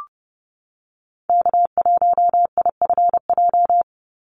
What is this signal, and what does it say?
Morse code, heard as a keyed tone, about 30 wpm, 700 Hz. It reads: K1IFJ
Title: Call Signs - 30wpm